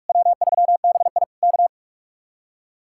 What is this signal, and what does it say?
Morse code, heard as a keyed tone, about 59 wpm, 700 Hz. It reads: W3BI K